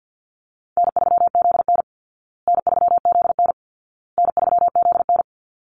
Morse code, heard as a keyed tone, about 50 wpm, 700 Hz. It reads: N3ZN N3ZN N3ZN
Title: Call Signs Repeated 2x - 50wpm